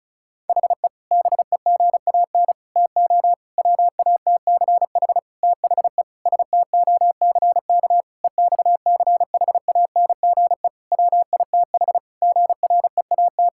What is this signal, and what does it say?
Morse code, heard as a keyed tone, about 36 wpm, 700 Hz. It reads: HE BEGAN TO WATCH THE STOCK EXCHANGE WITH GREAT
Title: Sentences from Top 1000 Words - Review Part 6 - 35wpm